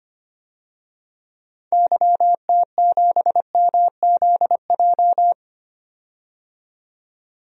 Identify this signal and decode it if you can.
Morse code, heard as a keyed tone, about 25 wpm, 700 Hz. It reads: YT7MZJ
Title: DXCC-100 Call Signs - 6 Characters - 25wpm